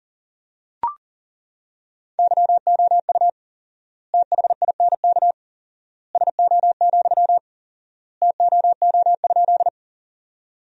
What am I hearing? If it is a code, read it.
Morse code, heard as a keyed tone, about 40 wpm, 700 Hz. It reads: YOU THINK SO, TOO?
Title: 4x Word-Spacing - Sentences from Top 200 Words - 40wpm